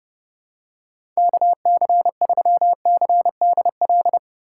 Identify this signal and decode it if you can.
Morse code, heard as a keyed tone, about 30 wpm, 700 Hz. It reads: KC3CDL
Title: USA Call Signs - 3x Character Spacing - 30wpm